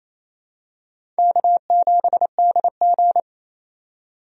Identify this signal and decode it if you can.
Morse code, heard as a keyed tone, about 28 wpm, 700 Hz. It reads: K7DG
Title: USA Call Signs - 4 Characters - 28wpm